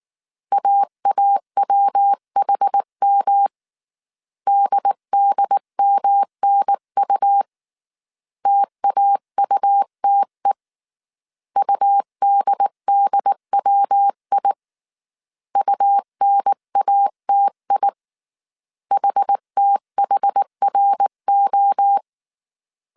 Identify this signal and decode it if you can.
Morse code, heard as a keyed tone, about 19 wpm, 800 Hz. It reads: AAWHM DDMNU TAUTE UDDWI UNATI HTHRO
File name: CW / DET / 1 3 1500C S B 18wpm 1916